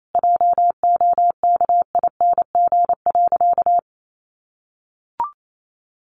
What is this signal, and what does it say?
Morse code, heard as a keyed tone, about 28 wpm, 700 Hz. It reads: JOKING.
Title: A Sound Of Thunder 28wpm